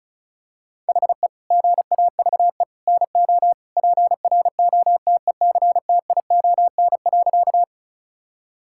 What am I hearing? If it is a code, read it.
Morse code, heard as a keyed tone, about 36 wpm, 700 Hz. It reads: HE GAVE NO PROTECTION.
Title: Sentences from Top 1600 Words 35wpm